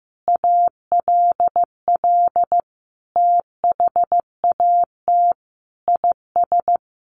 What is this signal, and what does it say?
Morse code, heard as a keyed tone, about 15 wpm, 700 Hz. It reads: ALL THAT IS
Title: Sentences from Top 600 Words 15wpm